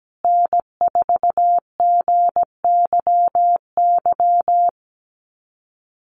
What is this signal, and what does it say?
Morse code, heard as a keyed tone, about 17 wpm, 700 Hz. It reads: N4GYY
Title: USA Call Signs Repeated 2x 17wpm